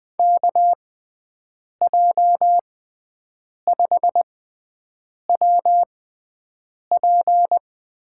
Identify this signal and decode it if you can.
Morse code, heard as a keyed tone, about 20 wpm, spaced as for 7 wpm, 700 Hz. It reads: KJ5WP
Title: USA Call Signs - 5 Characters - 6x Character Spacing - 20wpm